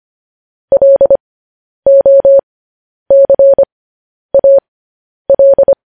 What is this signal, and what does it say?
Morse code, heard as a keyed tone, about 25 wpm, spaced as for 10 wpm, 550 Hz. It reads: LOCAL
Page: News Headlines in Morse Code at 10 WPM